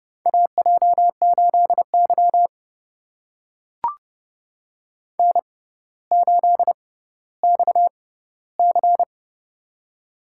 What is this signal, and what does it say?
Morse code, heard as a keyed tone, about 30 wpm, spaced as for 15 wpm, 700 Hz. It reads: AJ8Y N8XC
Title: USA Call Signs - 6x Character Spacing - 30wpm